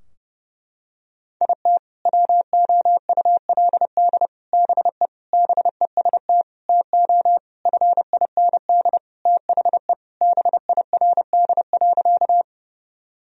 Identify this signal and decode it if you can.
Morse code, heard as a keyed tone, about 32 wpm, 700 Hz. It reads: IT WOULD BE BEST TO FIND THE BIRD.
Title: Sentences from Top 1000 Words - Review Part 2 - 30wpm